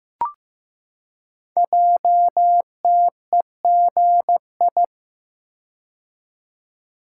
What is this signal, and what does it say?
Morse code, heard as a keyed tone, about 15 wpm, 700 Hz. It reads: JTEGI
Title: Code Groups - 5 Letters - 15wpm